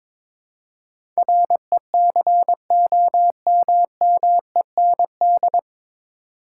Morse code, heard as a keyed tone, about 22 wpm, 700 Hz. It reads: RECOMMEND
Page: Morse Code Ninja practice 22wpm